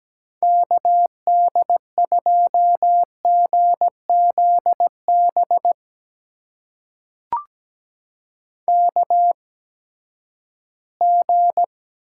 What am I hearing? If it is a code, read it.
Morse code, heard as a keyed tone, about 17 wpm, spaced as for 12 wpm, 700 Hz. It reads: KD2GZB KG
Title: USA Call Signs - 6 Characters - 8x Character Spacing - 17wpm